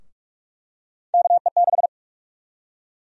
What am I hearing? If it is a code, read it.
Morse code, heard as a keyed tone, about 47 wpm, 700 Hz. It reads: KE6
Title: USA Call Sign Prefixes - Repeated 2x - 45wpm